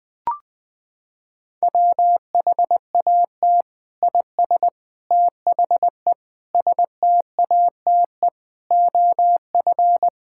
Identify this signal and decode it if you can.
Morse code, heard as a keyed tone, about 20 wpm, 700 Hz. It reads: WHAT IS THE STATE OF
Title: Sentences from Top 1000 Words - Review Part 5 - 20wpm